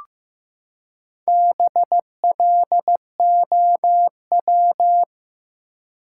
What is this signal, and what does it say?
Morse code, heard as a keyed tone, about 15 wpm, 700 Hz. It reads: BLOW
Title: Four Letter Words - 15wpm